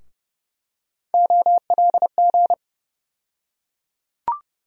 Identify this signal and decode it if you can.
Morse code, heard as a keyed tone, about 30 wpm, 700 Hz. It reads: OLG
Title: USA Call Sign Postfixes - 3 Letters 30wpm